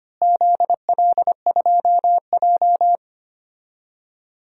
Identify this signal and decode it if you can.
Morse code, heard as a keyed tone, about 25 wpm, 700 Hz. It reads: ZL2J
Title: DXCC-100 Call Signs - 4 Characters - Repeated 3x - 25wpm